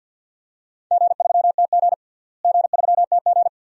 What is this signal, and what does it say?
Morse code, heard as a keyed tone, about 46 wpm, 700 Hz. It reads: G3TG G3TG
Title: DXCC-100 Call Signs - 4 Characters - Repeated 2x - 50wpm